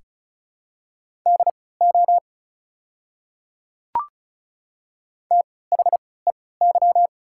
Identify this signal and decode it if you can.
Morse code, heard as a keyed tone, about 36 wpm, spaced as for 20 wpm, 700 Hz. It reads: DO THEY
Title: 3x Character Spacing - Top 100 Words - 35wpm